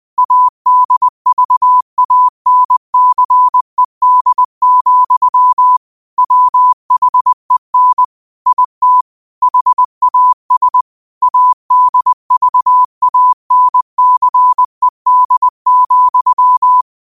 Morse code, heard as a keyed tone, about 20 wpm, 1000 Hz. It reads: ADVANCED, WHEN IT HAS ADVANCED,
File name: qotdfiles / 20wpm